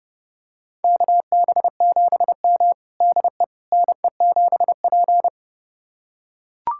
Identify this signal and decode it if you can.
Morse code, heard as a keyed tone, about 32 wpm, 700 Hz. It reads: KB7M DE NE7P
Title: USA Callsign DE Callsign - 4 Characters - 30wpm